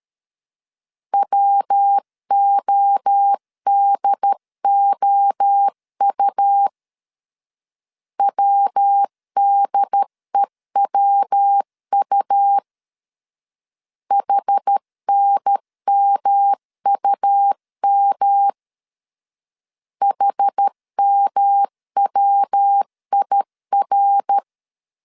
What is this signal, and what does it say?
Morse code, heard as a keyed tone, about 13 wpm, 800 Hz. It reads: WODOU WDEWU HNMUM HMWIR
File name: CW / DET / 1 3 1500C S A 12wpm 2905